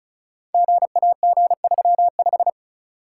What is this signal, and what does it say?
Morse code, heard as a keyed tone, about 36 wpm, 700 Hz. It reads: GAG35
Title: Code Groups - 5 Letter:Numbers - 35wpm